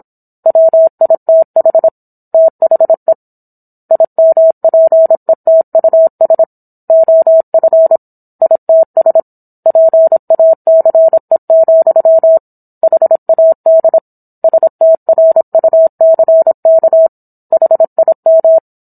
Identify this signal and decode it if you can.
Morse code, heard as a keyed tone, about 27 wpm, 650 Hz. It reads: WITH THE IMPETUS OF ITS PACE, HAD STRUCK HIM